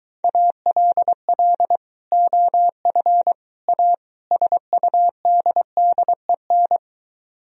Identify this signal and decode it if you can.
Morse code, heard as a keyed tone, about 23 wpm, 700 Hz. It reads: ALL OF A SUDDEN